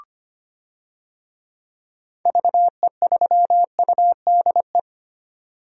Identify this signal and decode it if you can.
Morse code, heard as a keyed tone, about 25 wpm, 700 Hz. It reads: VE3UDE
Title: DXCC-100 Call Signs - 6 Characters - 25wpm